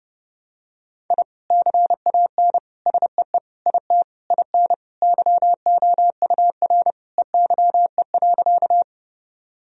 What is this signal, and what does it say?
Morse code, heard as a keyed tone, about 30 wpm, 700 Hz. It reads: I CAN SEE IT IN YOUR EYE.
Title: Sentences from Top 1000 Words - Review Part 1 - 30wpm